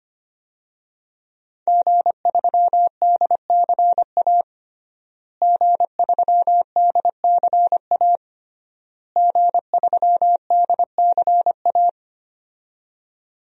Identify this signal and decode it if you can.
Morse code, heard as a keyed tone, about 25 wpm, 700 Hz. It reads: G3DCA G3DCA G3DCA
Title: DXCC-100 Call Signs - 5 Characters - Repeated 3x - 25wpm